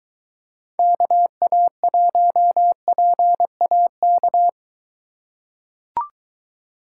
Morse code, heard as a keyed tone, about 21 wpm, 700 Hz. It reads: KA1PAK